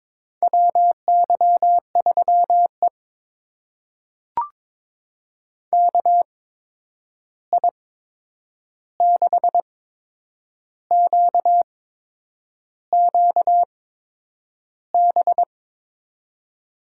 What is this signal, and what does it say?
Morse code, heard as a keyed tone, about 22 wpm, spaced as for 8 wpm, 700 Hz. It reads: WY3E KI6QQB
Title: USA Call Signs - 8x Character Spacing - 22wpm